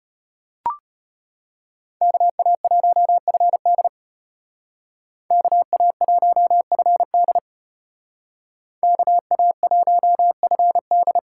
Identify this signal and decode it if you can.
Morse code, heard as a keyed tone, about 32 wpm, 700 Hz. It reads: KA1FD KA1FD KA1FD